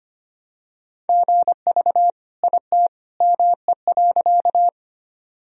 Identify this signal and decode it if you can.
Morse code, heard as a keyed tone, about 25 wpm, 700 Hz. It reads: GV IT ME.
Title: Sentences from Top 200 Words and Abbreviations 25wpm